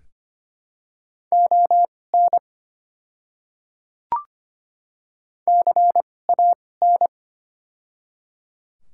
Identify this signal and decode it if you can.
Morse code, heard as a keyed tone, about 25 wpm, spaced as for 18 wpm, 700 Hz. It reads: ON CAN
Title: 2x Character Spacing - Top 500 Words - 25wpm